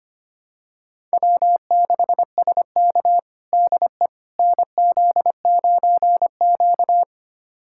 Morse code, heard as a keyed tone, about 25 wpm, 700 Hz. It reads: W6SK DE NZ9Q
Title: USA Callsign DE Callsign - 4 Characters - 25wpm